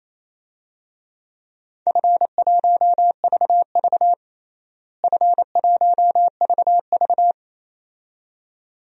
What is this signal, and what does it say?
Morse code, heard as a keyed tone, about 28 wpm, 700 Hz. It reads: F1VV F1VV
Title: DXCC-100 Call Signs - 4 Characters - Repeated 2x - 28wpm